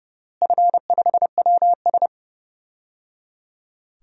Morse code, heard as a keyed tone, about 30 wpm, 700 Hz. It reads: F5WS
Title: DXCC-100 Call Signs - 4 Characters - Repeated 3x - 30wpm